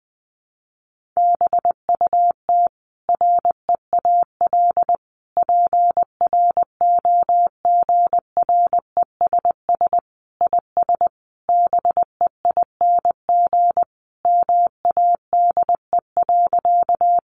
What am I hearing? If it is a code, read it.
Morse code, heard as a keyed tone, about 20 wpm, 700 Hz. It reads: BUT REAL PROGRESS IS BEING MADE.
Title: Sentences from Top 1400 Words 20wpm